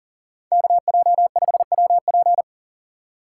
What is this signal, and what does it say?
Morse code, heard as a keyed tone, about 37 wpm, 700 Hz. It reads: KJ5WP
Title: USA Call Signs - 5 Characters - 2x Character Spacing - 40wpm